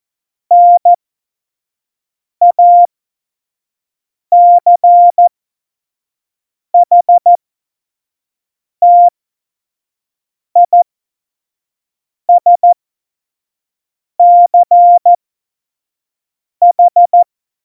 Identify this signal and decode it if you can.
Morse code, heard as a keyed tone, about 14 wpm, spaced as for 5 wpm, 700 Hz. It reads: NACHTISCH
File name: Morsekurs / Witze / Witz 05. Juni 2018, 25 F70